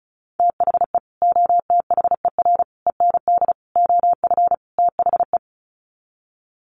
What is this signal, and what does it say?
Morse code, heard as a keyed tone, about 34 wpm, 700 Hz. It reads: THE OTHER END OF THE
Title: Sets of 5 Words 35wpm